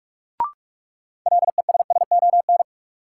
Morse code, heard as a keyed tone, about 46 wpm, 700 Hz. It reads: LESSON
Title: Sentences from Top 1200 Words - New Word Drill Alt - 45wpm